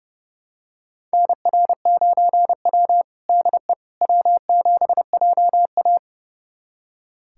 Morse code, heard as a keyed tone, about 30 wpm, 700 Hz. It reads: NR9W DE W7JA
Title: USA Callsign DE Callsign - 4 Characters - 30wpm